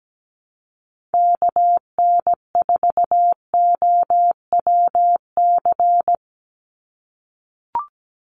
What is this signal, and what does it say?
Morse code, heard as a keyed tone, about 17 wpm, 700 Hz. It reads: KN4OWC
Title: USA Call Signs 17wpm